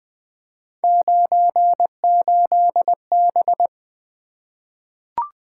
Morse code, heard as a keyed tone, about 20 wpm, 700 Hz. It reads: 98B
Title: Three Letter-Number Combinations - 20wpm